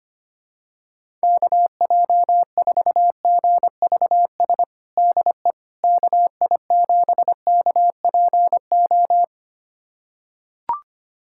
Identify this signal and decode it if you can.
Morse code, heard as a keyed tone, about 25 wpm, 700 Hz. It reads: KJ4GVS DE KI7KPO
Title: USA Callsign DE Callsign - 6 Characters - 25wpm